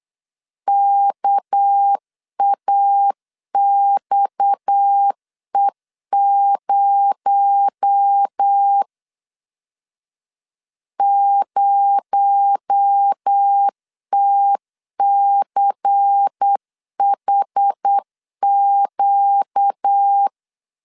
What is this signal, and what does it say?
Morse code, heard as a keyed tone, about 8 wpm, 800 Hz. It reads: KAXE0 0TCHQ
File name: LEZIONI / 6 PKX0 / 6 PKX0 8wpm 3341